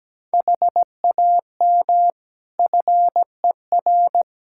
Morse code, heard as a keyed tone, about 17 wpm, 700 Hz. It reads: HAM FER
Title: Practice QSO - Exchange 3 - More Info - 17wpm